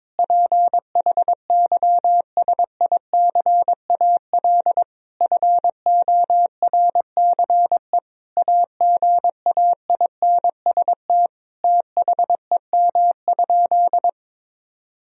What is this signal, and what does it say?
Morse code, heard as a keyed tone, about 22 wpm, 700 Hz. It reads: PHYSICAL FORCE AGAINST THEM?
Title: Sentences from Top 1000 Words - Review Part 4 - 22wpm